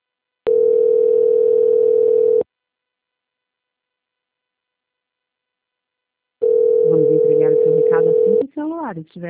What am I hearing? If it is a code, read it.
Ringback tone (North America), 440 + 480 Hz, with speech.
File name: out- -2001- - - .12018